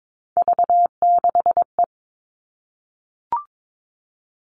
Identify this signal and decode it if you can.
Morse code, heard as a keyed tone, about 22 wpm, 700 Hz. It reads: V6E
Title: All Cut Numbers - 3-Digit 22wpm